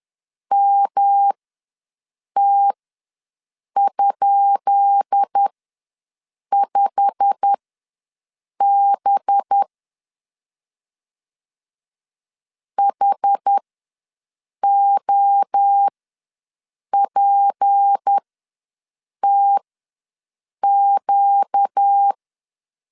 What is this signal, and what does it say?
Morse code, heard as a keyed tone, about 10 wpm, spaced as for 6 wpm, 800 Hz. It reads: MT?5B HOPTQ
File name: LEZIONI / 10 SIGNS / 10B INT ws 10wpm 4134